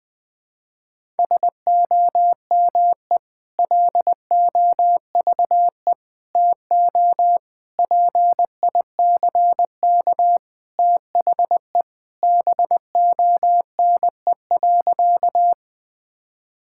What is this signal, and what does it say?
Morse code, heard as a keyed tone, about 20 wpm, 700 Hz. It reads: SOME LOVE TO PICK THE BONE.
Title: Sentences from Top 1600 Words 20wpm